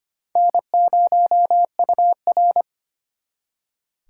Morse code, heard as a keyed tone, about 31 wpm, 700 Hz. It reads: N0UR
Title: Speed Racing - USA Call Signs 25wpm